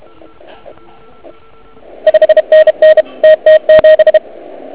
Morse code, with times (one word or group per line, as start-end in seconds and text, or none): 2.1-3.0 5NN
3.2-4.2 TT7